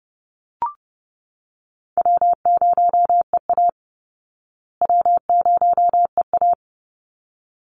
Morse code, heard as a keyed tone, about 30 wpm, 700 Hz. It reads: W0EA W0EA
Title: Call Signs Repeated 3x - 30wpm